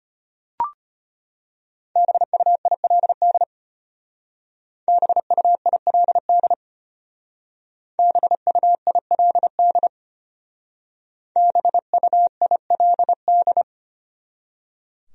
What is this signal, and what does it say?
Morse code, heard as a keyed tone, about 31 wpm, 700 Hz. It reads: BUILD BUILD BUILD BUILD
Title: Speed-Racing Top 500 Words 25wpm